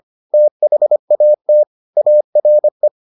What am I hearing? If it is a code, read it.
Morse code, heard as a keyed tone, about 25 wpm, 600 Hz. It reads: THAT ARE